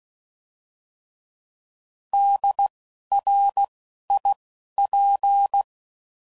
Morse code, heard as a keyed tone, about 16 wpm, spaced as for 11 wpm, 800 Hz. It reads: DRIP
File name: Prefixes / By First Letter / Ds